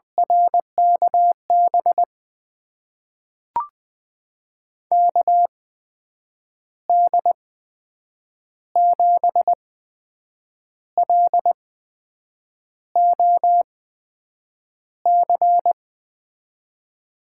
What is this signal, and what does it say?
Morse code, heard as a keyed tone, about 20 wpm, spaced as for 7 wpm, 700 Hz. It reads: RKB KD7LOC
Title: USA Call Signs - 6 Characters - 8x Character Spacing - 20wpm